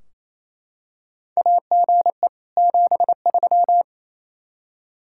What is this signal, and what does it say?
Morse code, heard as a keyed tone, about 28 wpm, 700 Hz. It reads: AGE 73
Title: Practice QSO - Exchange 3 - More Info - 28wpm